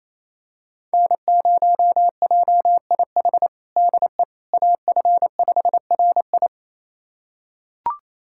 Morse code, heard as a keyed tone, about 27 wpm, 700 Hz. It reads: N0JIH DE AF5RI
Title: USA Callsign DE Callsign - 5 Characters - 28wpm